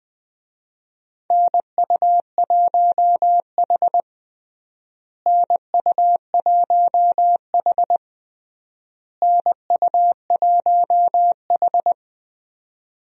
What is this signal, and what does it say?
Morse code, heard as a keyed tone, about 20 wpm, 700 Hz. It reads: NU1H NU1H NU1H
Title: USA Call Signs - 4 Characters - Repeated 3x - 20wpm